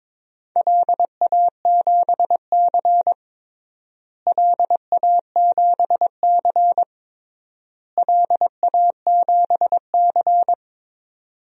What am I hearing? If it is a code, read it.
Morse code, heard as a keyed tone, about 22 wpm, 700 Hz. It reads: LA7C LA7C LA7C
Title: DXCC-100 Call Signs - 4 Characters - Repeated 3x - 22wpm